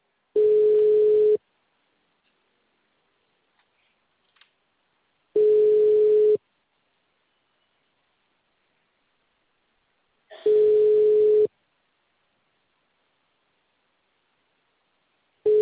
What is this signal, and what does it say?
Ringback tone (Europe, 425 Hz).
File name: out- -2006